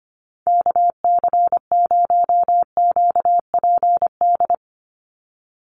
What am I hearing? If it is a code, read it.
Morse code, heard as a keyed tone, about 25 wpm, 690 Hz. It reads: KC0QPD
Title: USA Call Signs 25wpm